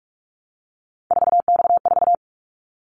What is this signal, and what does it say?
Morse code, heard as a keyed tone, about 44 wpm, 700 Hz. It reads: 4X4
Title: Callsign Prefixes for Top 50 DXCC - 45wpm